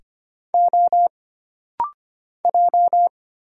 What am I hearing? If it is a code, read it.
Morse code, heard as a keyed tone, about 25 wpm, 700 Hz. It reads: O J
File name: Single Character (Letter:Number:Punctuation:Prosign) - Rapid-Fire - 25wpm